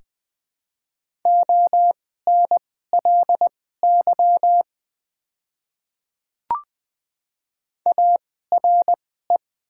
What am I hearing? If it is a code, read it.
Morse code, heard as a keyed tone, about 20 wpm, spaced as for 14 wpm, 700 Hz. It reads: ONLY ARE
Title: 2x Character Spacing - Top 200 Words - 20wpm